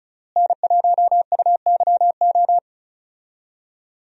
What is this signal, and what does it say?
Morse code, heard as a keyed tone, about 36 wpm, 700 Hz. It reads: N1UYO
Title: USA Call Signs - 5 Characters - Repeated 3x - 35wpm